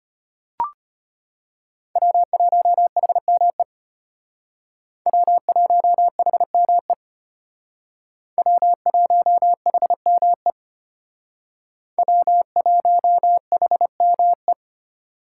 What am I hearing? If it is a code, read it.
Morse code, heard as a keyed tone, about 33 wpm, 700 Hz. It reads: W1HME W1HME W1HME W1HME